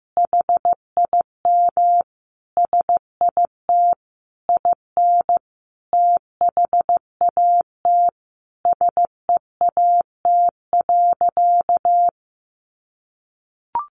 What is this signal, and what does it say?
Morse code, heard as a keyed tone, about 15 wpm, 700 Hz. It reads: HIM SIT IN THAT SEAT.
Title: Sentences from Top 1500 Words 15wpm